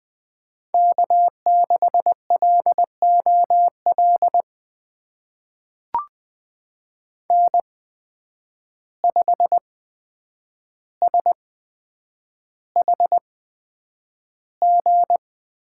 Morse code, heard as a keyed tone, about 20 wpm, spaced as for 9 wpm, 700 Hz. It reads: K6LOL N5SHG
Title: USA Call Signs - 5 Characters - 8x Character Spacing - 20wpm